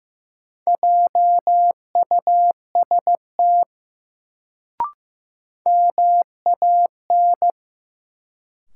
Morse code, heard as a keyed tone, about 15 wpm, 700 Hz. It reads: JUST MAN
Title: Sentences from Top 200 Words - New Word Drill - 15wpm